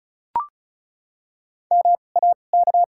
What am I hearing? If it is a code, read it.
Morse code, heard as a keyed tone, about 36 wpm, spaced as for 25 wpm, 700 Hz. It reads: MAK
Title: 2x Character Spacing - Top 100 Words - 35wpm